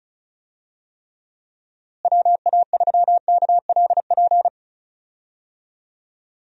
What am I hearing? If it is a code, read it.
Morse code, heard as a keyed tone, about 34 wpm, 700 Hz. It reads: WA3KLP
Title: USA Call Signs - 6 Characters - Repeated 2x - 35wpm